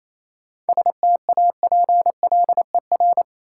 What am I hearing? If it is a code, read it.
Morse code, heard as a keyed tone, about 27 wpm, 690 Hz. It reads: STAPLER
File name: Easy Words - Speed-Racing with Hints - 22wpm